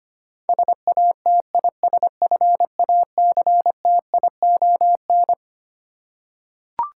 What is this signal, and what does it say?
Morse code, heard as a keyed tone, about 25 wpm, 700 Hz. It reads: SATISFACTION
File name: Twelve Letter Words - 25wpm